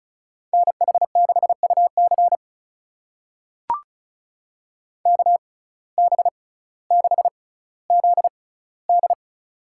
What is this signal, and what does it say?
Morse code, heard as a keyed tone, about 34 wpm, spaced as for 18 wpm, 700 Hz. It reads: NH6UC KB6ZD
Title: USA Call Signs - 5 Characters - 6x Character Spacing - 35wpm